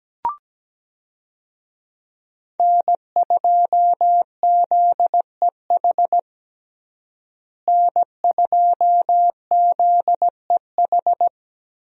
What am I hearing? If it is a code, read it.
Morse code, heard as a keyed tone, about 17 wpm, 700 Hz. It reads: N2ZEH N2ZEH
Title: USA Call Signs - 5 Characters - Repeated 2x - 17wpm